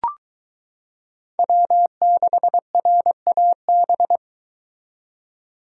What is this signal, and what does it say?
Morse code, heard as a keyed tone, about 22 wpm, 700 Hz. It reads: W6RAB